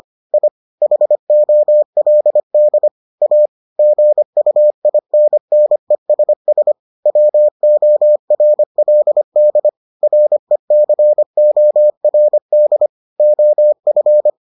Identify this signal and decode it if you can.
Morse code, heard as a keyed tone, about 25 wpm, 600 Hz. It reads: I HOLD A GUINNESS WORLD RECORD OF